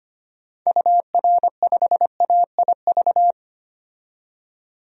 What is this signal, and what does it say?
Morse code, heard as a keyed tone, about 25 wpm, 700 Hz. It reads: UR5AIV
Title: DXCC-100 Call Signs - 6 Characters - 25wpm